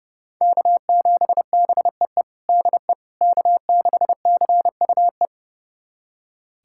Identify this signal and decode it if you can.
Morse code, heard as a keyed tone, about 30 wpm, 700 Hz. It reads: K7BEE DE K6CUE
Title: USA Callsign DE Callsign - 5 Characters - 30wpm